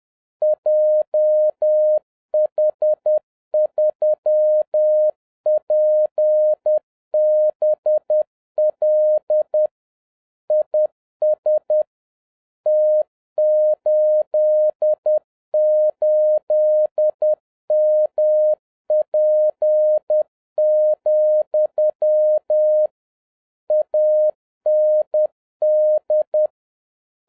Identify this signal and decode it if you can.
Morse code, heard as a keyed tone, about 10 wpm, 610 Hz. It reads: JH3PBL IS T88MP, AND